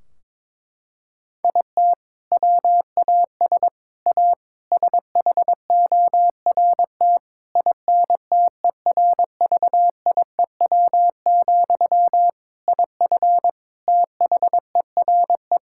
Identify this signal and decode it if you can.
Morse code, heard as a keyed tone, about 22 wpm, 700 Hz. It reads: IT WAS A SHORT INTERVIEW, IF THERE
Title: Sentences from Top 1600 Words 22wpm